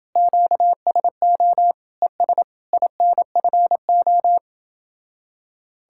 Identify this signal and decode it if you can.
Morse code, heard as a keyed tone, about 30 wpm, 700 Hz. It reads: QSO ES INFO